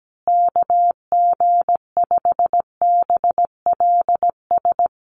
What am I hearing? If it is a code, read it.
Morse code, heard as a keyed tone, about 17 wpm, 700 Hz. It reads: KG5BLS
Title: USA Call Signs Repeated 2x 17wpm